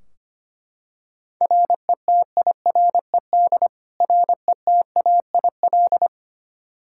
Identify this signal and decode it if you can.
Morse code, heard as a keyed tone, about 25 wpm, 700 Hz. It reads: RETIRED RETAIL
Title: K1USN SST QSO Practice 25wpm